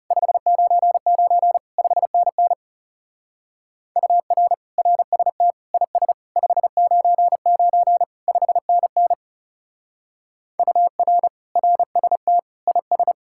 Morse code, heard as a keyed tone, about 37 wpm, 700 Hz. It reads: 599 5NN UR RST IS 599 5NN UR RST IS
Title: Speed-Racing Common QSO Phrases - 30wpm